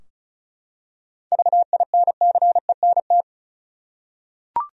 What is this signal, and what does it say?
Morse code, heard as a keyed tone, about 36 wpm, 700 Hz. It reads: VINCENT
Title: USA Common Names - 35wpm